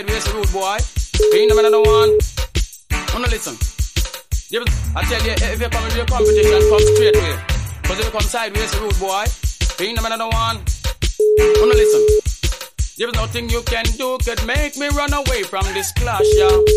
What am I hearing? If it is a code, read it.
Ringback tone (Europe, 425 Hz), with speech.